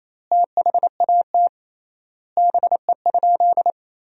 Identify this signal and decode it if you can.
Morse code, heard as a keyed tone, about 28 wpm, 700 Hz. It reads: THAT BE?
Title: 3x Word-Spacing - Sentences from Top 100 Words - 28wpm